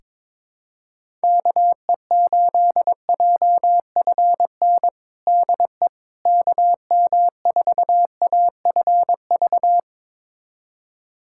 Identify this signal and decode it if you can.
Morse code, heard as a keyed tone, about 22 wpm, 700 Hz. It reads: KE8JFN DE KM4AFV
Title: USA Callsign DE Callsign - 6 Characters - 22wpm